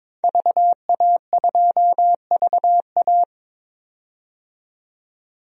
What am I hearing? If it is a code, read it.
Morse code, heard as a keyed tone, about 22 wpm, 700 Hz. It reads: VA2VA
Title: DXCC-100 Call Signs - 5 Characters - 22wpm